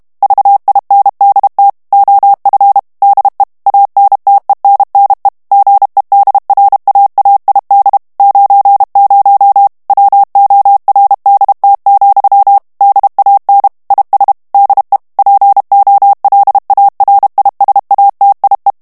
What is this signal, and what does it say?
Morse code, heard as a keyed tone, about 32 wpm, 800 Hz. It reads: VINDT OF DE ANTENNE GEDRAAID 90 WORDT, DAN IS DE POLARISATIE
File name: morse mp3 / HOLLANDAIS texte clair 1 vitesse 30 mots minute